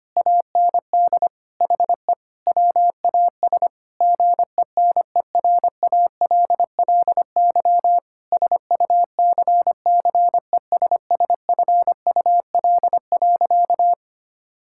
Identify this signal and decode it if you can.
Morse code, heard as a keyed tone, about 25 wpm, 700 Hz. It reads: AND HE WAS GENERALLY SUCCESSFUL.
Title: Sentences from Top 1000 Words - Review Part 6 - 25wpm